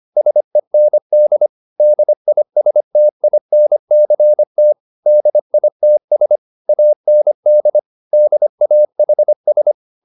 Morse code, heard as a keyed tone, about 25 wpm, 600 Hz. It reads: SEND DISTINCT DITS AND DAHS